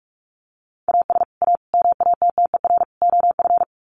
Morse code, heard as a keyed tone, about 44 wpm, 700 Hz. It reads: AS A MATTER OF
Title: Sets of 4 Words 45wpm